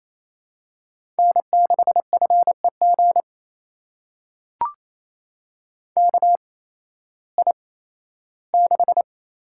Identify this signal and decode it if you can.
Morse code, heard as a keyed tone, about 28 wpm, spaced as for 15 wpm, 700 Hz. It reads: N6FEG KI6
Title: USA Call Signs - 5 Characters - 8x Character Spacing - 28wpm